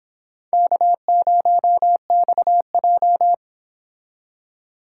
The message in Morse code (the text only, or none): K0XJ